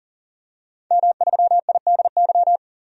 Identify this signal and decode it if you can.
Morse code, heard as a keyed tone, about 40 wpm, 700 Hz. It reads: M3IDY
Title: DXCC-100 Call Signs - 5 Characters - 40wpm